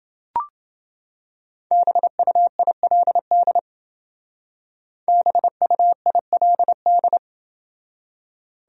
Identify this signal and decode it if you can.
Morse code, heard as a keyed tone, about 28 wpm, 700 Hz. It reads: BUILD BUILD